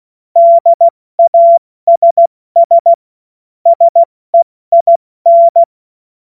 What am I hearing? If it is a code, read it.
Morse code, heard as a keyed tone, about 16 wpm, 680 Hz. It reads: DASS SEIN
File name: Morsekurs / Witze / Witz 25. Oktober 2022 70 F80